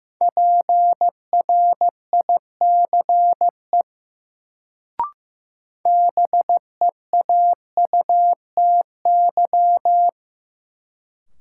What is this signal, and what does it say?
Morse code, heard as a keyed tone, about 15 wpm, 700 Hz. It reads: PRICE BEAUTY
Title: Sentences from Top 600 Words - New Word Drill - 15wpm